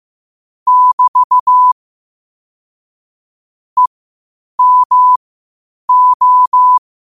Morse code, heard as a keyed tone, about 15 wpm, spaced as for 8 wpm, 1000 Hz. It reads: <BT> EMO
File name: qotdfiles / 8wpm